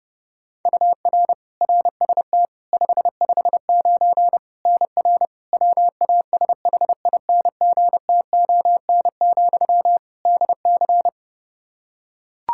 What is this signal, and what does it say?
Morse code, heard as a keyed tone, about 32 wpm, 700 Hz. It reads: UR RST 559 NR WASHINGTON, DC
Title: K1USN SST QSO Practice 30wpm